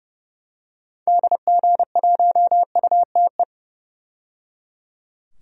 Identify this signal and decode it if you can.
Morse code, heard as a keyed tone, about 30 wpm, 700 Hz. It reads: DG1UTE
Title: DXCC-100 Call Signs - 6 Characters - 30wpm